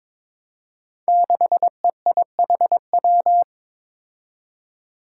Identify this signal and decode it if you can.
Morse code, heard as a keyed tone, about 22 wpm, 700 Hz. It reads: 6EIHW
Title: Five Challenging Letter-Number Combinations 22wpm